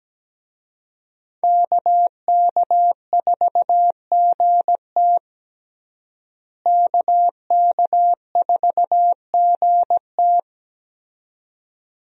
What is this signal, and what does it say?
Morse code, heard as a keyed tone, about 17 wpm, 700 Hz. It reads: KK4GT KK4GT
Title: USA Call Signs - 5 Characters - Repeated 2x - 17wpm